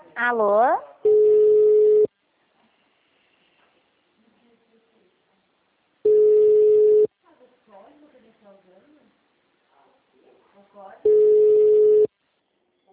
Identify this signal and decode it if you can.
Ringback tone (Europe, 425 Hz), with speech.